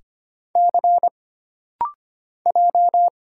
Morse code, heard as a keyed tone, about 25 wpm, 700 Hz. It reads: C J
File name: Single Character (Letter:Number:Punctuation:Prosign) - Rapid-Fire - 25wpm